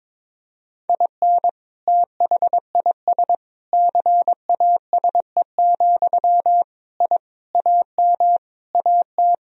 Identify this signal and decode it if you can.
Morse code, heard as a keyed tone, about 21 wpm, 700 Hz. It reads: IN THIS CASE, I AM AT
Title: Sentences from Top 1000 Words - Review Part 4 - 22wpm